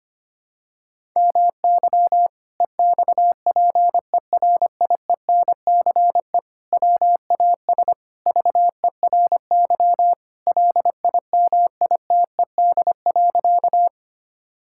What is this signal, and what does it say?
Morse code, heard as a keyed tone, about 25 wpm, 700 Hz. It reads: MY EXPERIENCE WAS VERY LIMITED.
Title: Sentences from Top 1000 Words - Review Part 6 - 25wpm